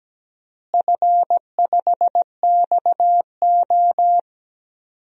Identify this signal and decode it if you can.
Morse code, heard as a keyed tone, about 17 wpm, 700 Hz. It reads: F5XO
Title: DXCC-100 Call Signs - 4 Characters - Repeated 2x - 17wpm